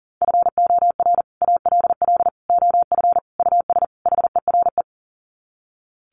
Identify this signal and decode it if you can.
Morse code, heard as a keyed tone, about 40 wpm, 700 Hz. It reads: FOR ALL OF US HERE
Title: Sets of 5 Words 40wpm